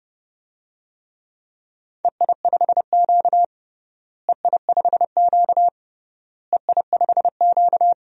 Morse code, heard as a keyed tone, about 32 wpm, 700 Hz. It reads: EI5Q EI5Q EI5Q
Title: DXCC-100 Call Signs - 4 Characters - Repeated 3x - 30wpm